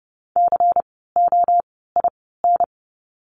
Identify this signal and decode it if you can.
Morse code, heard as a keyed tone, about 30 wpm, 700 Hz. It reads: C O I N
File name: Lesson 16f - Letter c - Words 30wpm